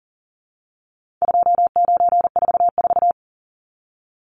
Morse code, heard as a keyed tone, about 40 wpm, 700 Hz. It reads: 2944
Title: Numbers - 4-Digit 40wpm